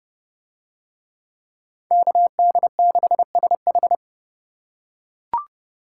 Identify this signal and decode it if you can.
Morse code, heard as a keyed tone, about 30 wpm, 700 Hz. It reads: KD6SH